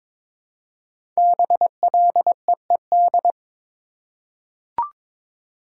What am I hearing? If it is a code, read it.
Morse code, heard as a keyed tone, about 22 wpm, 700 Hz. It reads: BLEED
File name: 5 Letter Words - First 3 Letters Given - 22wpm